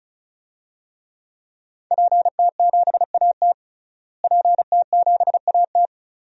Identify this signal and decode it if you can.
Morse code, heard as a keyed tone, about 34 wpm, 700 Hz. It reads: PT7AT PT7AT
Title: DXCC-100 Call Signs - 5 Characters - Repeated 2x - 35wpm